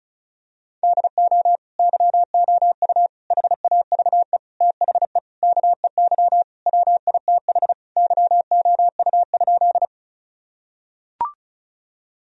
Morse code, heard as a keyed tone, about 36 wpm, 700 Hz. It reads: DO YOU HAVE THE KEY WITH YOU?
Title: Sentences from Top 1000 Words - Review Part 4 - 35wpm